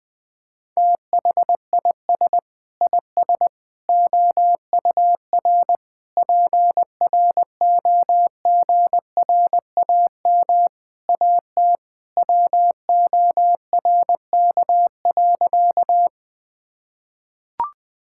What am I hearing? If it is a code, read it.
Morse code, heard as a keyed tone, about 21 wpm, 700 Hz. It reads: THIS IS OUR PROGRAM AT WORK.
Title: Sentences from Top 1000 Words - Review Part 5 - 20wpm